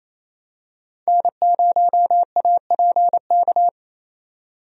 Morse code, heard as a keyed tone, about 28 wpm, 700 Hz. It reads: N0APK
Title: USA Call Signs - 5 Characters - 28wpm